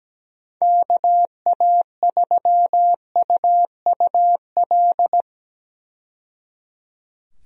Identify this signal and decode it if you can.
Morse code, heard as a keyed tone, about 17 wpm, 700 Hz. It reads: KA3UUL
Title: USA Call Signs - 6 Characters - 17wpm